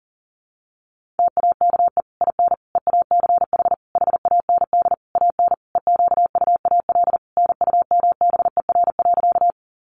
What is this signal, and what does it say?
Morse code, heard as a keyed tone, about 40 wpm, 700 Hz. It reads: TAKE IN EACH HAND AN EQUAL NUMBER.
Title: Sentences from Top 1400 Words 40wpm